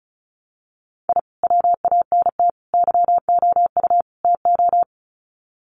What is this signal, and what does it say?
Morse code, heard as a keyed tone, about 34 wpm, 700 Hz. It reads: I WANT YOU TO
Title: Sets of 4 Words 35wpm